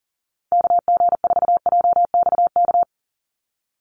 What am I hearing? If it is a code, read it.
Morse code, heard as a keyed tone, about 40 wpm, 690 Hz. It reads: KG4JXK
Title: USA Call Signs 40wpm